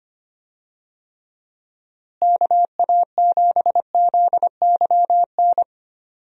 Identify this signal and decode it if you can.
Morse code, heard as a keyed tone, about 25 wpm, 700 Hz. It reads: KA7ZYN
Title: USA Call Signs - 6 Characters - Repeated 2x - 25wpm